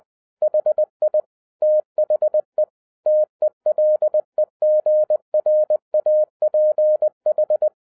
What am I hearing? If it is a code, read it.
Morse code, heard as a keyed tone, about 20 wpm, 610 Hz. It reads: HI THE TELEGRAPH
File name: 2011-10-mp3 / n13-TIPS FROM MY 34 YEARS OF EXPERIENCE 20wpm